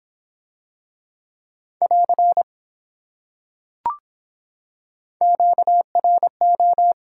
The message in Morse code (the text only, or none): <AR> QRO